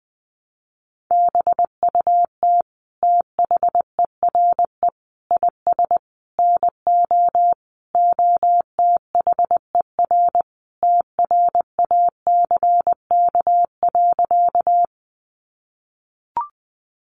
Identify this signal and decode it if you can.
Morse code, heard as a keyed tone, about 20 wpm, 700 Hz. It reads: BUT THERE IS NO OTHER TRACK.
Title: Sentences from Top 1100 Words 20wpm